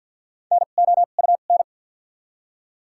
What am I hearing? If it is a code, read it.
Morse code, heard as a keyed tone, about 53 wpm, spaced as for 37 wpm, 700 Hz. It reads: NOUN
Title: 2x Character Spacing - Top 500 Words - 50wpm